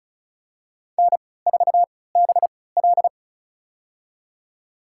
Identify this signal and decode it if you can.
Morse code, heard as a keyed tone, about 34 wpm, spaced as for 20 wpm, 700 Hz. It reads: N4BL
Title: USA Call Signs - 4 Characters - 3x Character Spacing - 35wpm